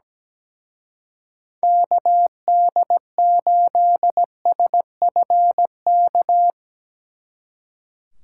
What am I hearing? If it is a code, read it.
Morse code, heard as a keyed tone, about 20 wpm, 700 Hz. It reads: KD8SFK
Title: Speed Racing - USA 2x3 Call Signs 17wpm